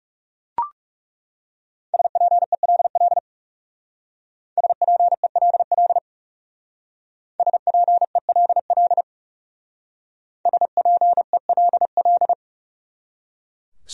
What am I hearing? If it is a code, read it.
Morse code, heard as a keyed tone, about 37 wpm, 700 Hz. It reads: SPELL SPELL SPELL SPELL
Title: Speed-Racing Top 500 Words 30wpm